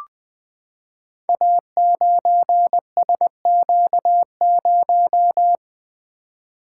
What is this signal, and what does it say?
Morse code, heard as a keyed tone, about 20 wpm, 700 Hz. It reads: A9SQ0
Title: Code Groups - 5 Letter:Numbers - 20wpm